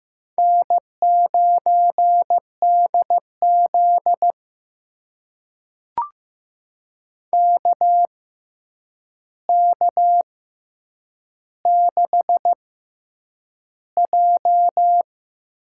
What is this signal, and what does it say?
Morse code, heard as a keyed tone, about 15 wpm, spaced as for 8 wpm, 700 Hz. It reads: N9DZ KK6J
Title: USA Call Signs - 6x Character Spacing - 15wpm